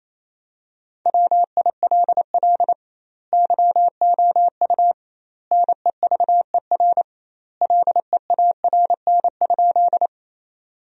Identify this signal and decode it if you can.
Morse code, heard as a keyed tone, about 28 wpm, 700 Hz. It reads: WILL YOU NEVER LEARN?
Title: 2x Word-Spacing - Sentences from Top 300 Words - 28wpm